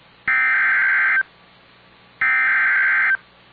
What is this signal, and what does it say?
Emergency Alert System (SAME) tones (header bursts), heard as harsh buzzing bursts. It is heard from 0.3 to 3.1 s.